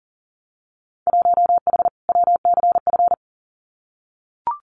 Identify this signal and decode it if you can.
Morse code, heard as a keyed tone, about 38 wpm, 700 Hz. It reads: 1H WCF
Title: Winter Field Day Exchanges 40wpm